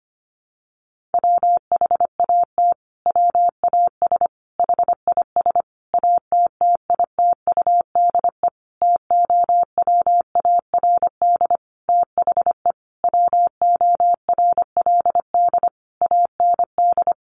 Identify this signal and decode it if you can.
Morse code, heard as a keyed tone, about 25 wpm, 700 Hz. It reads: WHAT WAS HIS ATTITUDE TOWARD THE WORLD AND
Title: Sentences from Top 1300 Words 25wpm